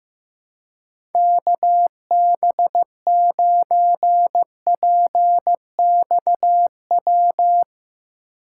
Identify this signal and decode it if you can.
Morse code, heard as a keyed tone, about 15 wpm, 700 Hz. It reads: KB9PXW
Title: Speed Racing - USA 2x3 Call Signs 15wpm